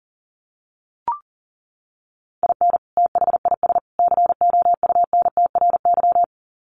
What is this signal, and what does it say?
Morse code, heard as a keyed tone, about 40 wpm, 700 Hz. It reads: IN THIS COUNTRY
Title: Sets of 3 Words 40wpm